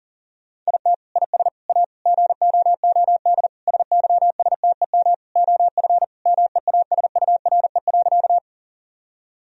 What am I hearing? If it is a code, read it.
Morse code, heard as a keyed tone, about 37 wpm, 700 Hz. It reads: IT IS A GOOD SYSTEM OF MEASURE.
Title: Sentences from Top 1000 Words - Review Part 3 - 40wpm